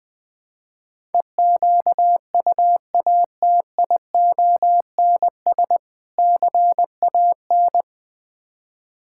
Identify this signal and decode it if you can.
Morse code, heard as a keyed tone, about 21 wpm, 700 Hz. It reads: EQUATIONS CAN
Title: Build A Phrase - 5-Word Phrases - 20wpm